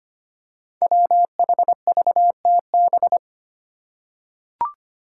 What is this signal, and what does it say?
Morse code, heard as a keyed tone, about 25 wpm, 700 Hz. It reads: WHVTB
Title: Five Challenging Letter-Number Combinations 25wpm